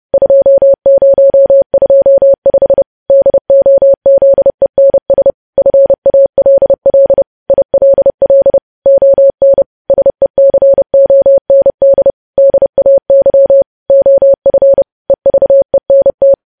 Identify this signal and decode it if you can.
Morse code, heard as a keyed tone, about 30 wpm, 550 Hz. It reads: 2025 DOZENS FALL ILL ON SECOND DAY OF EVENT
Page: news headlines in morse code at 30 wpm